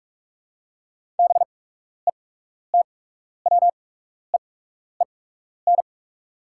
Morse code, heard as a keyed tone, about 44 wpm, spaced as for 12 wpm, 690 Hz. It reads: BETWEEN
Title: 8x Character Spacing - Top 300 Words - 45wpm